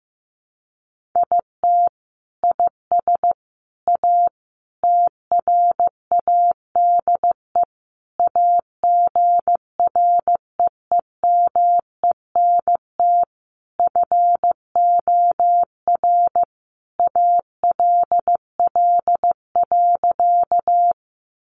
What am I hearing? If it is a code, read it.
Morse code, heard as a keyed tone, about 15 wpm, 700 Hz. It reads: IT IS A TRADE AGREEMENT FOR ALL.
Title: Sentences from Top 1500 Words 15wpm